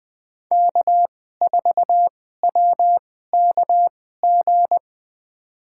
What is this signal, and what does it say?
Morse code, heard as a keyed tone, about 20 wpm, spaced as for 14 wpm, 700 Hz. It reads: K4WKG
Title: USA Call Signs - 2x Character Spacing - 20wpm